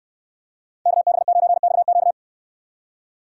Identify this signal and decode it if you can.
Morse code, heard as a keyed tone, about 69 wpm, 700 Hz. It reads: KB9XQ